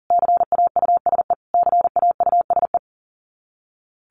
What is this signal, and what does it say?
Morse code, heard as a keyed tone, about 40 wpm, 700 Hz. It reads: CAUSE CAUSE
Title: Top 200 Words Repeated 40wpm